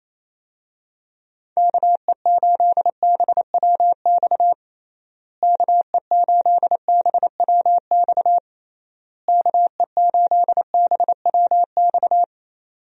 Morse code, heard as a keyed tone, about 28 wpm, 700 Hz. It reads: KE8BWX KE8BWX KE8BWX
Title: USA Call Signs - 6 Characters - Repeated 3x - 28wpm